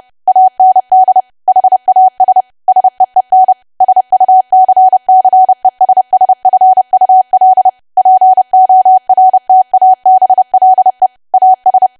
Morse code, heard as a keyed tone, about 30 wpm, 750 Hz. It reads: AND HAS SEEN SUCCESSFUL PORTABLE AS
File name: cwa / W1AW-Level3 / 30WPM